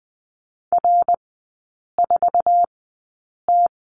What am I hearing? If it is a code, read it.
Morse code, heard as a keyed tone, about 20 wpm, spaced as for 8 wpm, 700 Hz. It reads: R4T
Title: 053 Lesson 10e - Letter r - Three Characters 20wpm